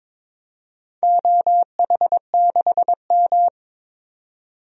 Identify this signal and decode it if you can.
Morse code, heard as a keyed tone, about 22 wpm, 700 Hz. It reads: OH6M
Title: DXCC-100 Call Signs - 4 Characters - 22wpm